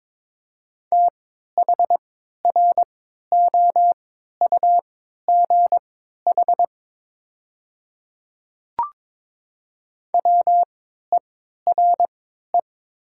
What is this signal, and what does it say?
Morse code, heard as a keyed tone, about 22 wpm, spaced as for 13 wpm, 700 Hz. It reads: THROUGH WERE
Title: 3x Character Spacing - Top 200 Words - 22wpm